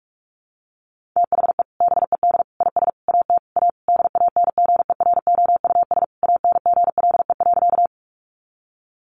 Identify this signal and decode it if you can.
Morse code, heard as a keyed tone, about 44 wpm, 700 Hz. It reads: THE BED IS AT A DANGEROUS ANGLE.
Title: Sentences from Top 1500 Words 45wpm